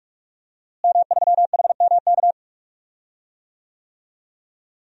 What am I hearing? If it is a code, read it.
Morse code, heard as a keyed tone, about 41 wpm, 700 Hz. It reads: M3HMK
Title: DXCC-100 Call Signs - 5 Characters - Repeated 2x - 45wpm